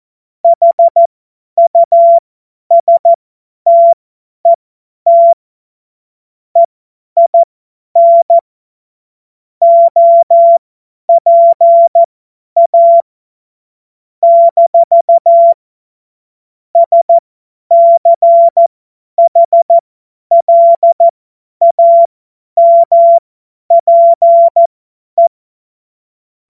Morse code, heard as a keyed tone, about 14 wpm, spaced as for 10 wpm, 680 Hz. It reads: HUSTET EIN OPA - SCHLAMPE
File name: Morsekurs / Witze / Witz 30. November 2021, 50 F70